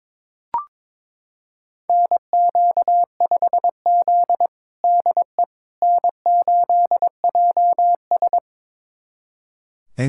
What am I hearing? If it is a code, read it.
Morse code, heard as a keyed tone, about 22 wpm, 700 Hz. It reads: NQ5Z DE N8JS
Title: USA Callsign DE Callsign - 4 Characters - 22wpm